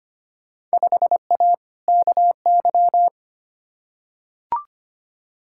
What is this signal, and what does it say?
Morse code, heard as a keyed tone, about 25 wpm, 700 Hz. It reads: 5A KY
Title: Field Day Exchanges - 25wpm